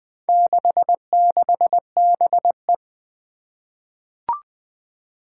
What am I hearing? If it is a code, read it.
Morse code, heard as a keyed tone, about 20 wpm, 700 Hz. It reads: 66BE
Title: Four Challenging Letter-Number Combinations 20wpm